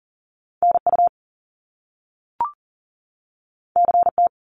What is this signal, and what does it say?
Morse code, heard as a keyed tone, about 41 wpm, 700 Hz. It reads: NU CT
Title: Top Two Letter Combinations in Words 40wpm